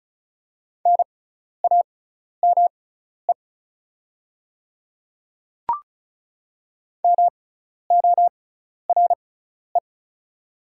Morse code, heard as a keyed tone, about 36 wpm, spaced as for 12 wpm, 700 Hz. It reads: NAME MORE
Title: 6x Character Spacing - Top 200 Words - 35wpm